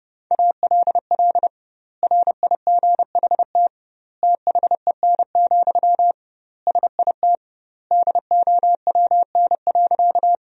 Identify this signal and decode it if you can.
Morse code, heard as a keyed tone, about 30 wpm, 700 Hz. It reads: ALL RIGHT THEN, SIT DOWN.
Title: 2x Word-Spacing - Sentences from Top 500 Words - 30wpm